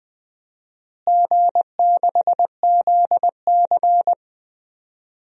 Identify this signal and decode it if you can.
Morse code, heard as a keyed tone, about 20 wpm, 700 Hz. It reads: G6ZC
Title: DXCC-100 Call Signs - 4 Characters - Repeated 2x - 20wpm